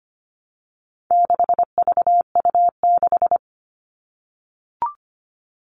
Morse code, heard as a keyed tone, about 25 wpm, 700 Hz. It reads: 6VU6
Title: All Cut Numbers - 4-Digit 25wpm